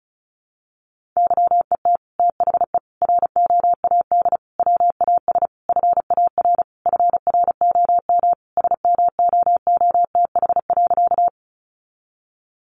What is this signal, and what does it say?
Morse code, heard as a keyed tone, about 34 wpm, 700 Hz. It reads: YET THE ROAD WAS FAR FROM SMOOTH.
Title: Sentences from Top 1400 Words 35wpm